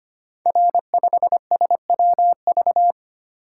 Morse code, heard as a keyed tone, about 25 wpm, 700 Hz. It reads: R5SWV
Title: Code Groups - 5 Letter:Numbers - 25wpm